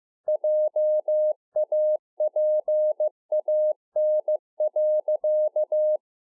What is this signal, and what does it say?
Morse code, heard as a keyed tone, about 15 wpm, 610 Hz. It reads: JAPAN.